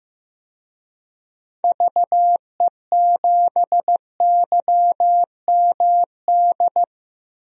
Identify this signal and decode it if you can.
Morse code, heard as a keyed tone, about 15 wpm, 700 Hz. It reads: VE7YMD
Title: DXCC-100 Call Signs - 6 Characters - Repeated 2x - 15wpm